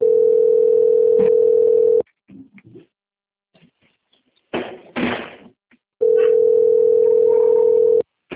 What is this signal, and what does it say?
Ringback tone (North America), 440 + 480 Hz.
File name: out- - - - - .23064